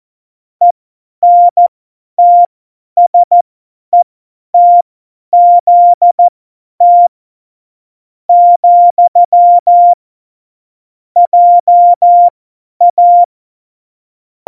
Morse code, heard as a keyed tone, about 14 wpm, spaced as for 9 wpm, 700 Hz. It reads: ENTSETZT, JA
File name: Morsekurs / Witze / Witz 13. Oktober 2020, 50 F70